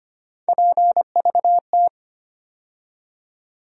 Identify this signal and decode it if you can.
Morse code, heard as a keyed tone, about 25 wpm, 700 Hz. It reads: PVT
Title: USA Call Sign Postfixes - 3 Letters 25wpm